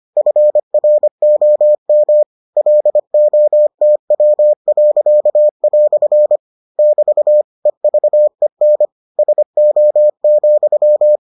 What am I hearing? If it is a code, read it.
Morse code, heard as a keyed tone, about 25 wpm, 600 Hz. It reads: FROM LOTW." <BT> EVEN SO,